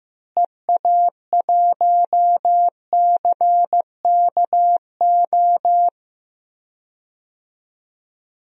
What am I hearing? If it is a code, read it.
Morse code, heard as a keyed tone, about 16 wpm, 700 Hz. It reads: EA1CKO
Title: DXCC-100 Call Signs - 6 Characters - Repeated 2x - 15wpm